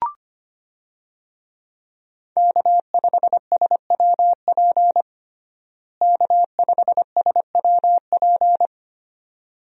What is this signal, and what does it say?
Morse code, heard as a keyed tone, about 25 wpm, 700 Hz. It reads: K5SWP K5SWP
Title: USA Call Signs - 5 Characters - Repeated 2x - 25wpm